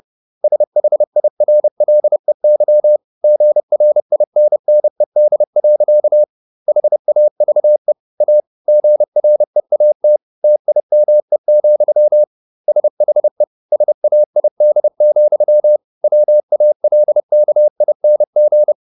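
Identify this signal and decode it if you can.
Morse code, heard as a keyed tone, about 30 wpm, 600 Hz. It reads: SHIRLEY GRINNED. HAVE A GREAT TIME, SHE SAID, WALKING
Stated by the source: Morse code at 30 wpm, the novel Never by Ken Follett in English, from CW med Gnister.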